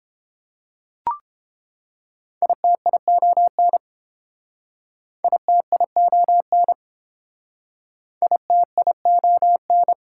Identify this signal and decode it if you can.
Morse code, heard as a keyed tone, about 30 wpm, 700 Hz. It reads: ITION ITION ITION